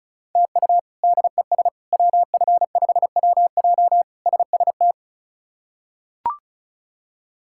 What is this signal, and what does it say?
Morse code, heard as a keyed tone, about 36 wpm, 700 Hz. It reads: TU DES WF5WJ SST
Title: K1USN SST QSO Practice - 35wpm